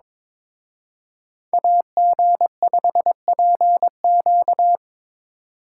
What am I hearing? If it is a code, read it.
Morse code, heard as a keyed tone, about 26 wpm, 700 Hz. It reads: AG5PQ
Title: Speed Racing - USA 2x2 Call Signs 22wpm